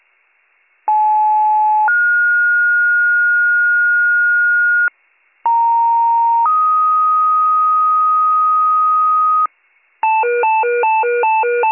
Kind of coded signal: two-tone page

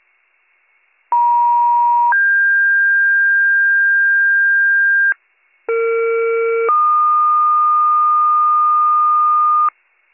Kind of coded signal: two-tone page

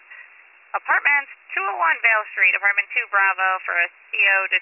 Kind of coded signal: Emergency Alert System (SAME) tones

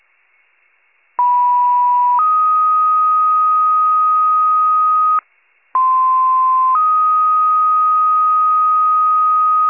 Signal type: two-tone page

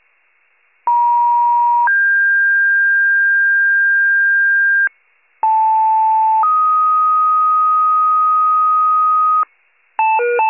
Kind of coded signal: two-tone page

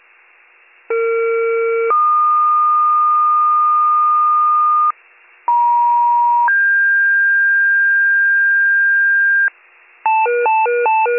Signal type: two-tone page

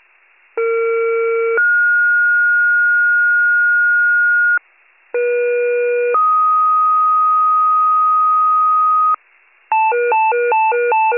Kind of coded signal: two-tone page

